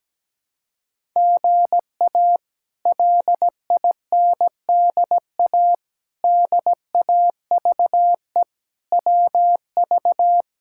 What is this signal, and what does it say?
Morse code, heard as a keyed tone, about 17 wpm, 700 Hz. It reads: GA LINDA DAVE WV
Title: K1USN SST QSO Practice - 17wpm